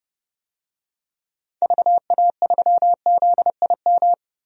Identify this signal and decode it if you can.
Morse code, heard as a keyed tone, about 31 wpm, 700 Hz. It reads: VA3ZIM
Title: DXCC-100 Call Signs - 6 Characters - 30wpm